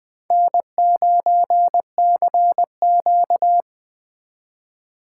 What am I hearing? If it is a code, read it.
Morse code, heard as a keyed tone, about 25 wpm, 700 Hz. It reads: N9CQ
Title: Speed Racing - USA 1x2 Call Signs 20wpm